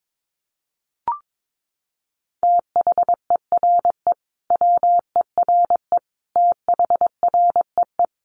Morse code, heard as a keyed tone, about 22 wpm, 700 Hz. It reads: THERE WERE THREE
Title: Sentences from Top 800 Words 22wpm